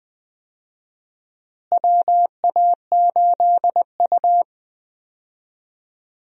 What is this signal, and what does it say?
Morse code, heard as a keyed tone, about 20 wpm, 700 Hz. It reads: WA8U
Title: USA Call Signs - 4 Characters - Repeated 2x - 20wpm